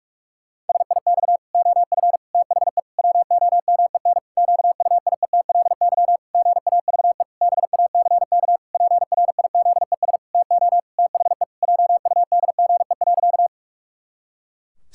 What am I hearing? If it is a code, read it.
Morse code, heard as a keyed tone, about 47 wpm, 700 Hz. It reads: SIX OF THE WOMEN QUIETLY GAVE BACK PRIZES TO THE JUDGE.
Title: Pangram Challenge v2 - 45wpm